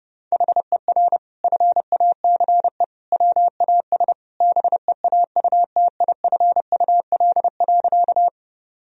Morse code, heard as a keyed tone, about 30 wpm, 700 Hz. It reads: HER FACE WAS BEAUTIFUL.
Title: Sentences from Top 1000 Words - Review Part 2 - 30wpm